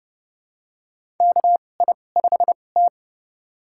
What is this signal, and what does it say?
Morse code, heard as a keyed tone, about 30 wpm, spaced as for 22 wpm, 700 Hz. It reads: KI5T
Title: USA Call Signs - 4 Characters - 2x Character Spacing - 30wpm